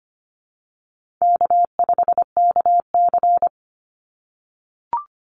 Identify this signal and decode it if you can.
Morse code, heard as a keyed tone, about 25 wpm, 700 Hz. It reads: K5KC
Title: Easy Call Signs - 25wpm